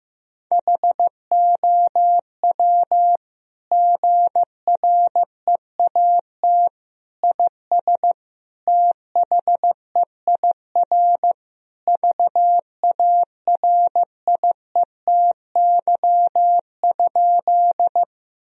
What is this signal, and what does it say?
Morse code, heard as a keyed tone, about 15 wpm, 700 Hz. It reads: HOW GREAT IS THEIR VARIETY?
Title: Sentences from Top 1000 Words - Review Part 4 - 15wpm